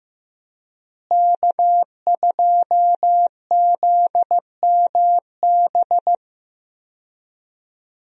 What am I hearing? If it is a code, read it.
Morse code, heard as a keyed tone, about 15 wpm, 700 Hz. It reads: K2ZMB
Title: USA Call Signs - 5 Characters - Repeated 3x - 15wpm